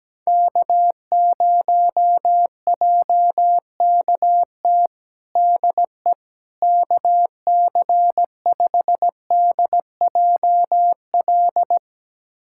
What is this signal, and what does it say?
Morse code, heard as a keyed tone, about 17 wpm, 700 Hz. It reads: K0JKT DE KC5DJL
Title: USA Callsign DE Callsign 17wpm